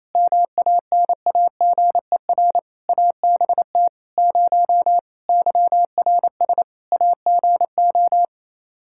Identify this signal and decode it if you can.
Morse code, heard as a keyed tone, about 27 wpm, 700 Hz. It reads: MANAGER ABT 0 YRS AGO
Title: Practice QSO - Exchange 3 - More Info - 28wpm